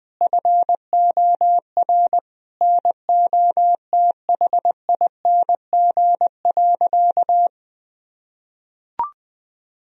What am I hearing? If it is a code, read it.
Morse code, heard as a keyed tone, about 20 wpm, 700 Hz. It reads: FOR NOTHING.
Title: Sentences from Top 1000 Words - Review Part 2 - 20wpm